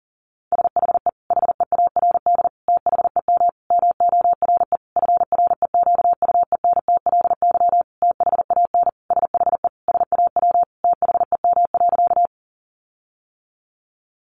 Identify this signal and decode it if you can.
Morse code, heard as a keyed tone, about 40 wpm, 700 Hz. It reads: SHE HEARD THEM MORE FREQUENTLY THAN SHE SAW THEM.
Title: Sentences from Top 1300 Words 40wpm